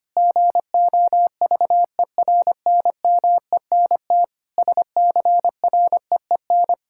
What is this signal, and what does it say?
Morse code, heard as a keyed tone, about 25 wpm, 700 Hz. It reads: GOVERNMENT SCREEN
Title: Sentences from Top 1000 Words - Review Part 6 - 25wpm